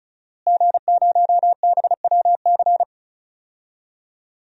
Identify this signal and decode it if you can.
Morse code, heard as a keyed tone, about 36 wpm, 700 Hz. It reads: G0BWC
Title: DXCC-100 Call Signs - 5 Characters - Repeated 3x - 35wpm